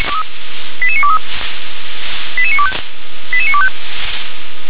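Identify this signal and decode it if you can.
Five-tone selective call (ZVEI), heard as quick runs of five tones, tones of about 70 ms. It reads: 88923 88925 88925
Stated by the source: SDR recording of a ZVEI five-tone selective call, address 88952.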